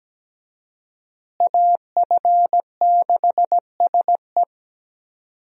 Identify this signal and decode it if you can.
Morse code, heard as a keyed tone, about 17 wpm, 700 Hz. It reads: AF6SE
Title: USA Call Signs - 5 Characters - Repeated 3x - 17wpm